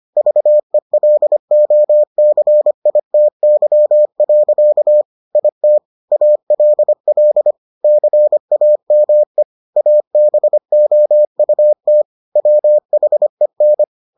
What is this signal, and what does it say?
Morse code, heard as a keyed tone, about 25 wpm, 600 Hz. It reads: VELOCITY. IT ALL CAME ABOUT WHEN